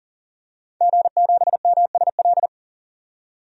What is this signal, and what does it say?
Morse code, heard as a keyed tone, about 40 wpm, 700 Hz. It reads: G7MSL
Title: DXCC-100 Call Signs - 5 Characters - 40wpm